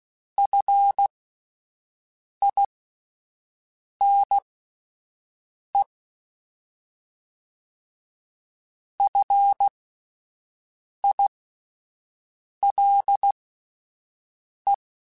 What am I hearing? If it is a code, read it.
Morse code, heard as a keyed tone, about 16 wpm, spaced as for 5 wpm, 800 Hz. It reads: FINE FILE
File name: Alphabetical / Alphabetical mp3s / Fs @5